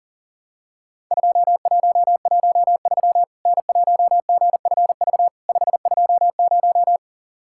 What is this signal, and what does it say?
Morse code, heard as a keyed tone, about 44 wpm, 700 Hz. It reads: 2113 N1GFV 520
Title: Worked All Europe DX Contest - QTC Exchange - 40wpm